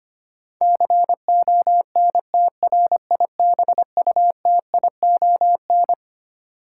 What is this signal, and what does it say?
Morse code, heard as a keyed tone, about 25 wpm, 700 Hz. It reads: CONTRIBUTION
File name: Twelve Letter Words - 25wpm